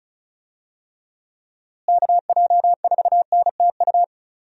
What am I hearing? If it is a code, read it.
Morse code, heard as a keyed tone, about 36 wpm, 700 Hz. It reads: KJ4NTU
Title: USA Call Signs - 6 Characters - Repeated 3x - 35wpm